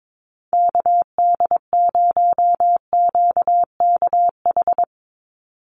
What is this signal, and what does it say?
Morse code, heard as a keyed tone, about 22 wpm, 700 Hz. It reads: KD0QKH
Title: USA Call Signs 22wpm